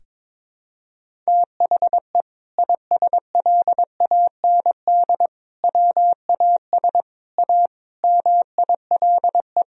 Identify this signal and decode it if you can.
Morse code, heard as a keyed tone, about 22 wpm, 700 Hz. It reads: THE ISLAND WAS A MILE
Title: Sentences from Top 500 Words and Abbreviations 22wpm